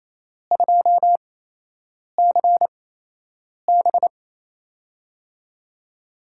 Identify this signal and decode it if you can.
Morse code, heard as a keyed tone, about 28 wpm, spaced as for 8 wpm, 700 Hz. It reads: 2CB
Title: USA Call Signs - 8x Character Spacing - 28wpm